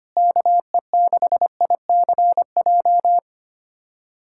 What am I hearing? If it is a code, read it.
Morse code, heard as a keyed tone, about 25 wpm, 700 Hz. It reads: KE6ICJ
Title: USA Callsign DE Callsign 25wpm